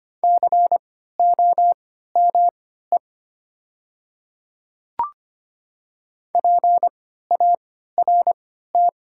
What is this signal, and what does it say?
Morse code, heard as a keyed tone, about 25 wpm, spaced as for 14 wpm, 700 Hz. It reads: COME PART
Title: 3x Character Spacing - Top 200 Words - 25wpm